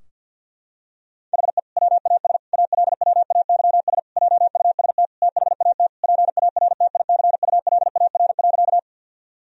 Morse code, heard as a keyed tone, about 52 wpm, 700 Hz. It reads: HE WAS ALWAYS JUST THAT PARTICULAR.
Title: Sentences from Top 1000 Words - Review Part 4 - 50wpm